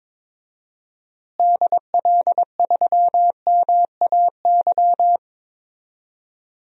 Morse code, heard as a keyed tone, about 22 wpm, 700 Hz. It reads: DL3MAY
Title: DXCC-100 Call Signs - 6 Characters - Repeated 3x - 22wpm